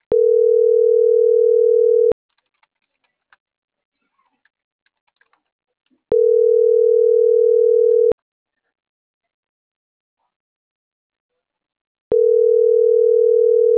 Ringback tone (North America), 440 + 480 Hz.
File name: CENTRAL MAYAB / out michelle.martinez